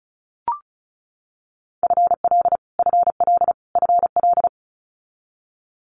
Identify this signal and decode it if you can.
Morse code, heard as a keyed tone, about 36 wpm, 700 Hz. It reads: FL FL FL
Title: Multi-letter Phonemes Repeated 3x 35wpm